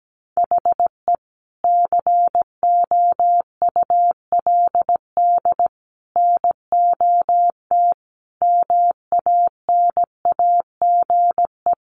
Morse code, heard as a keyed tone, about 17 wpm, 700 Hz. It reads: HE COULD NOT MANAGE
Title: Sentences from Top 1400 Words 17wpm